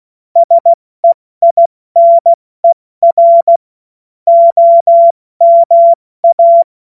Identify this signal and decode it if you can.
Morse code, heard as a keyed tone, about 16 wpm, spaced as for 14 wpm, 680 Hz. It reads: SEINER OMA
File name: Morsekurs / Witze / Witz 17. August 2021, 70 F80